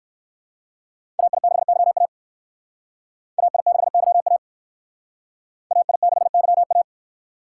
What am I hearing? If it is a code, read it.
Morse code, heard as a keyed tone, about 59 wpm, 700 Hz. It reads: AI6YA AI6YA AI6YA